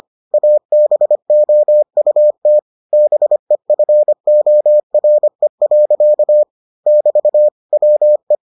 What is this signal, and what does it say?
Morse code, heard as a keyed tone, about 25 wpm, 600 Hz. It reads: ABOUT BEFORE. <BT> WE